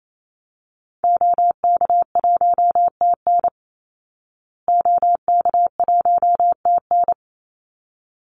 Morse code, heard as a keyed tone, about 28 wpm, 700 Hz. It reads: OK1TN OK1TN
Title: Call Signs Repeated 3x - 28wpm